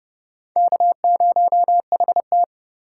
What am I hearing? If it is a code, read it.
Morse code, heard as a keyed tone, about 30 wpm, 700 Hz. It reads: K0HT
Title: K1USN SST QSO Practice - 30wpm